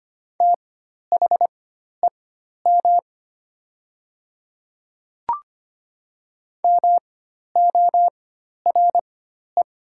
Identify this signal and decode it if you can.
Morse code, heard as a keyed tone, about 25 wpm, spaced as for 12 wpm, 700 Hz. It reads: THEM MORE
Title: 4x Character Spacing - Top 100 Words - 25wpm